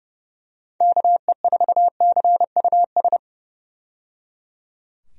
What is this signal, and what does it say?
Morse code, heard as a keyed tone, about 30 wpm, 700 Hz. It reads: KE4CUS
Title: USA Call Signs - 6 Characters - 30wpm